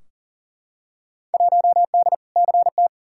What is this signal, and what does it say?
Morse code, heard as a keyed tone, about 40 wpm, 700 Hz. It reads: 1D CT
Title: Field Day Exchanges - 40wpm